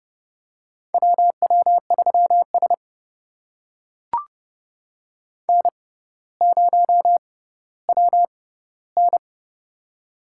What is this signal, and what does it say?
Morse code, heard as a keyed tone, about 31 wpm, spaced as for 15 wpm, 700 Hz. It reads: WW3S N0WN
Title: USA Call Signs - 4 Characters - 6x Character Spacing - 30wpm